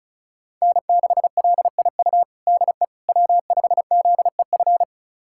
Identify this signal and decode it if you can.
Morse code, heard as a keyed tone, about 36 wpm, 700 Hz. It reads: N6LIU DE W5ZEF
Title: USA Callsign DE Callsign - 5 Characters - 35wpm